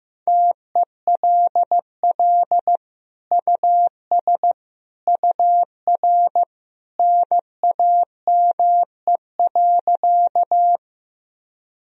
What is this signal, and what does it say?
Morse code, heard as a keyed tone, about 16 wpm, 700 Hz. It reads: TELL US UR NAME.
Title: Sentences from Top 200 Words and Abbreviations 15wpm